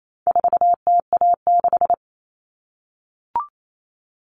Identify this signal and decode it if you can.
Morse code, heard as a keyed tone, about 28 wpm, 700 Hz. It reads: 4TA6
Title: All Cut Numbers - 4-Digit 28wpm